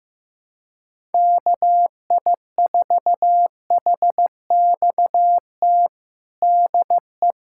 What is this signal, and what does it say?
Morse code, heard as a keyed tone, about 16 wpm, 700 Hz. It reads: KI4HXT DE
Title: USA Callsign DE Callsign - 6 Characters - 15wpm